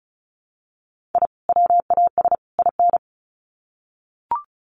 Morse code, heard as a keyed tone, about 34 wpm, 700 Hz. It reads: I WAS IN
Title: Sets of 3 Words 35wpm